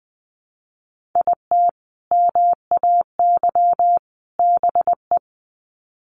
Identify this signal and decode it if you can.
Morse code, heard as a keyed tone, about 20 wpm, 700 Hz. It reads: IT MAY BE
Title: Sets of 3 Words 20wpm